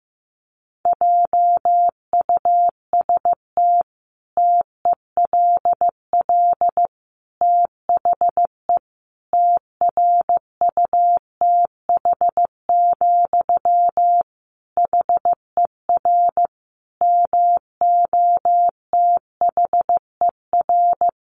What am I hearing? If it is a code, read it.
Morse code, heard as a keyed tone, about 14 wpm, 700 Hz. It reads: JUST TELL THE TRUTH, HER MOTHER
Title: Sentences from Top 800 Words 15wpm